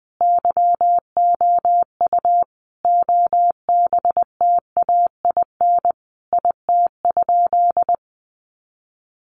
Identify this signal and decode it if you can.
Morse code, heard as a keyed tone, about 20 wpm, 700 Hz. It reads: YOU OBTAIN IT?
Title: Sentences from Top 1300 Words 20wpm